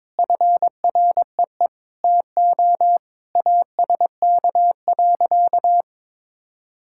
Morse code, heard as a keyed tone, about 22 wpm, 700 Hz. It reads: FREE TO ASK.
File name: practice / 22wpm / Sentences with 8-Word Phrases - 22wpm